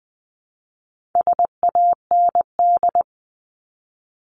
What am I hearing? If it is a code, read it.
Morse code, heard as a keyed tone, about 20 wpm, 700 Hz. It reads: SAND
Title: 260 Lesson 42e - Words 20-20wpm